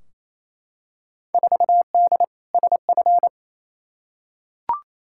Morse code, heard as a keyed tone, about 28 wpm, 700 Hz. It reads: 4D SF
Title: Field Day Exchanges Encore - 28wpm